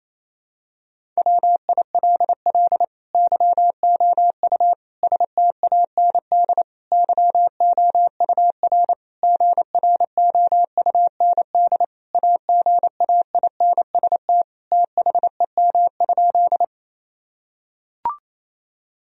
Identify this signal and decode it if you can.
Morse code, heard as a keyed tone, about 28 wpm, 700 Hz. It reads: WILL YOU STAND YOUR GROUND AGAINST THEM?
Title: Sentences from Top 1000 Words - Review Part 2 - 28wpm